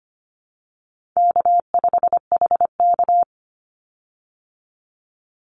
Morse code, heard as a keyed tone, about 25 wpm, 700 Hz. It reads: K5HK
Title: Easy Call Signs - 25wpm